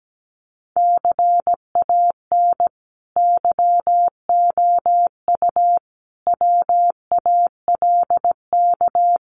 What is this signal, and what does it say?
Morse code, heard as a keyed tone, about 17 wpm, 700 Hz. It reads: CAN YOU WALK
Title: Sentences from Top 1000 Words 17wpm